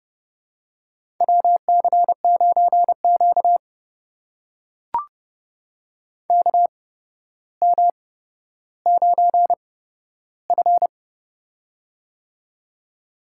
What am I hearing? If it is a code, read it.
Morse code, heard as a keyed tone, about 31 wpm, spaced as for 13 wpm, 700 Hz. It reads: WC9Q KM9F
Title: USA Call Signs - 4 Characters - 8x Character Spacing - 30wpm